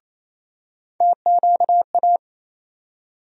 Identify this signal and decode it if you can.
Morse code, heard as a keyed tone, about 27 wpm, 700 Hz. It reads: TQA
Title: USA Call Sign Postfixes - 3 Letters 28wpm